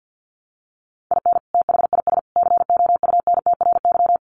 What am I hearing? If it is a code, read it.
Morse code, heard as a keyed tone, about 50 wpm, 700 Hz. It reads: IN THIS COUNTRY
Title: Sets of 3 Words 50wpm